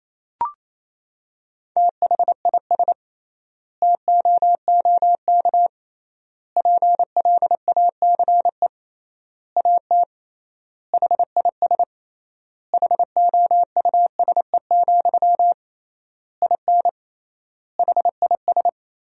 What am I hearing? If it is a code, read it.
Morse code, heard as a keyed tone, about 28 wpm, 700 Hz. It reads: THIS TOOK PLACE AT HIS HOUSE, IN HIS
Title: 3x Word-Spacing - Sentences from Top 300 Words - 28wpm